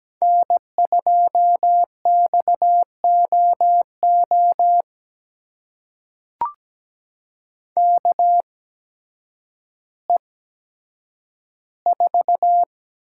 Morse code, heard as a keyed tone, about 17 wpm, spaced as for 9 wpm, 700 Hz. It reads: N2XOO KE4
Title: USA Call Signs - 5 Characters - 8x Character Spacing - 17wpm